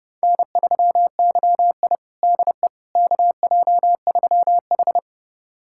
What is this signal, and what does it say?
Morse code, heard as a keyed tone, about 30 wpm, 700 Hz. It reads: N3YI DE KJ3H
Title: USA Callsign DE Callsign - 4 Characters - 30wpm